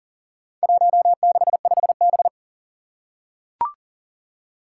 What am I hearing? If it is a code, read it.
Morse code, heard as a keyed tone, about 40 wpm, 700 Hz. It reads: 165B
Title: Four Challenging Letter-Number Combinations 40wpm